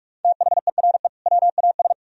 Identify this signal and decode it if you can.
Morse code, heard as a keyed tone, about 44 wpm, 700 Hz. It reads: THERE WAS